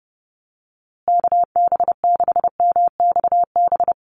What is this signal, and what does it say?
Morse code, heard as a keyed tone, about 30 wpm, 700 Hz. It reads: KB6MXB
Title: USA Call Signs Repeated 2x 30wpm